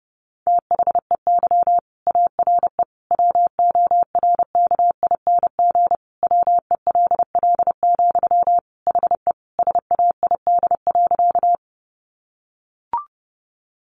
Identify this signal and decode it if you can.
Morse code, heard as a keyed tone, about 30 wpm, 700 Hz. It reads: THEY ARE WORKING WELL, HE SAID.
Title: Sentences from Top 700 Words 30wpm